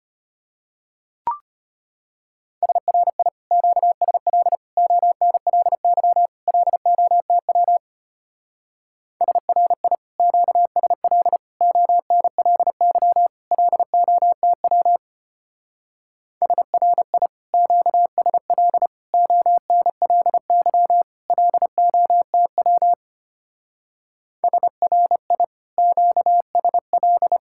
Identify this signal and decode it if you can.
Morse code, heard as a keyed tone, about 31 wpm, 700 Hz. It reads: SRI QSL ONLY LOTW SRI QSL ONLY LOTW SRI QSL ONLY LOTW SRI QSL
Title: Speed-Racing Common QSO Phrases - 25wpm